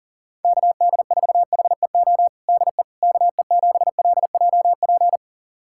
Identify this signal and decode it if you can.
Morse code, heard as a keyed tone, about 40 wpm, 700 Hz. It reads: KD4HEO DE KE7LJP
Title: USA Callsign DE Callsign 40wpm